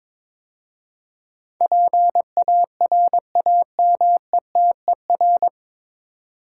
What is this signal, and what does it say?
Morse code, heard as a keyed tone, about 21 wpm, 700 Hz. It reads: PARAMETER
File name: 9 Letter Words - First 3 Letters Given - 22wpm